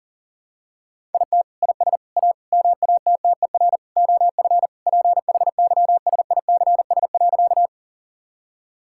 Morse code, heard as a keyed tone, about 40 wpm, 700 Hz. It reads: IT IS A MATTER OF PHYSICS.
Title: Sentences from Top 1000 Words - Review Part 5 - 40wpm